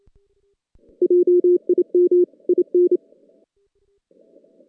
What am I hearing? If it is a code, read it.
Morse code, heard as a keyed tone, about 27 wpm, 410 Hz. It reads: JIM IN
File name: 2016 JAN CW NAQP